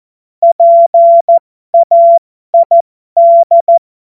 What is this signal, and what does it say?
Morse code, heard as a keyed tone, about 14 wpm, spaced as for 12 wpm, 680 Hz. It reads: PAID